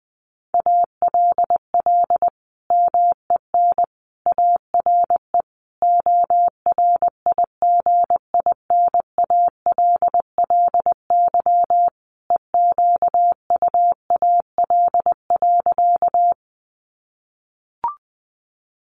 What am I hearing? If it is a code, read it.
Morse code, heard as a keyed tone, about 20 wpm, 700 Hz. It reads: ALL MEN ARE ORIGINALLY EQUAL.
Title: Sentences from Top 1500 Words 20wpm